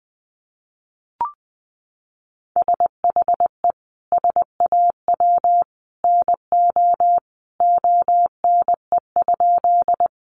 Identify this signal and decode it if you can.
Morse code, heard as a keyed tone, about 20 wpm, 700 Hz. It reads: SHE SAW NO ONE?
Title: Sentences from Top 300 Words 20wpm